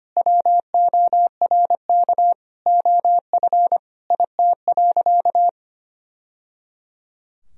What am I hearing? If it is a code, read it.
Morse code, heard as a keyed tone, about 25 wpm, 700 Hz. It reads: WORK OF IT.
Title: Sentences from Top 1000 Words - Review Part 2 - 25wpm